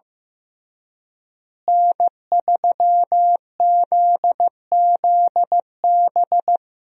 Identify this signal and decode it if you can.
Morse code, heard as a keyed tone, about 15 wpm, 700 Hz. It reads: N3ZZB
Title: USA Call Signs - 5 Characters - Repeated 3x - 15wpm